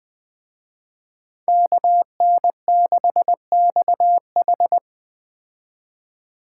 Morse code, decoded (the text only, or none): KN6XH